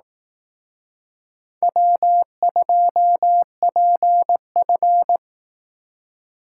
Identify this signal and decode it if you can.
Morse code, heard as a keyed tone, about 19 wpm, 700 Hz. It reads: W2PF
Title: Speed Racing - USA Call Signs 15wpm